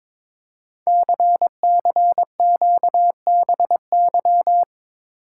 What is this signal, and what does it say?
Morse code, heard as a keyed tone, about 22 wpm, 700 Hz. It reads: CCQBY
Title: Code Groups - 5 Letters - 22wpm